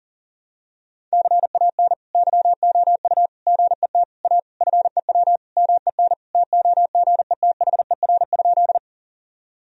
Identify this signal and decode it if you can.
Morse code, heard as a keyed tone, about 40 wpm, 700 Hz. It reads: CAN YOU GET A FEW MEN TOGETHER?
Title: Sentences from Top 1000 Words - Review Part 1 - 40wpm